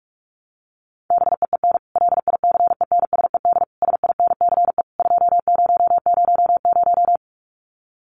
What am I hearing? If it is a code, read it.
Morse code, heard as a keyed tone, about 44 wpm, 700 Hz. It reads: BEEN LICENSED SINCE 2000
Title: Common QSO Phrases 45wpm